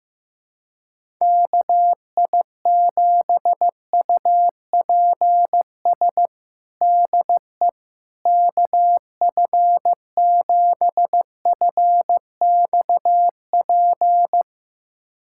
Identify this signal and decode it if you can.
Morse code, heard as a keyed tone, about 15 wpm, 700 Hz. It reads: KI7UPS DE KF7FXP
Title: USA Callsign DE Callsign - 6 Characters - 15wpm